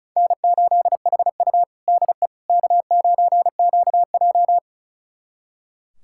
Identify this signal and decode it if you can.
Morse code, heard as a keyed tone, about 34 wpm, 700 Hz. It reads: N8HU DE K9QJ
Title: USA Callsign DE Callsign - 4 Characters - 35wpm